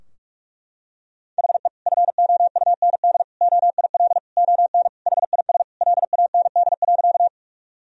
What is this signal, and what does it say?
Morse code, heard as a keyed tone, about 41 wpm, 700 Hz. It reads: HE FOUND OIL ON HIS LAND.
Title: Sentences from Top 1000 Words - Review Part 4 - 45wpm